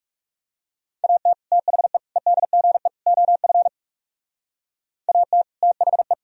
Morse code, heard as a keyed tone, about 38 wpm, 700 Hz. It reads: AT THE EDGE OF AT THE
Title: Speed-Racing - Sets of 4 Words 30wpm